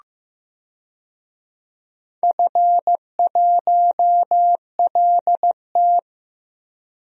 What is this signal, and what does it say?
Morse code, heard as a keyed tone, about 15 wpm, 700 Hz. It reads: F1LT
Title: DXCC-100 Call Signs - 4 Characters - 15wpm